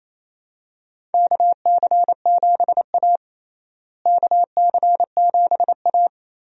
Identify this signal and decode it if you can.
Morse code, heard as a keyed tone, about 28 wpm, 700 Hz. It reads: KC7A KC7A
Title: USA Call Signs - 4 Characters - Repeated 2x - 28wpm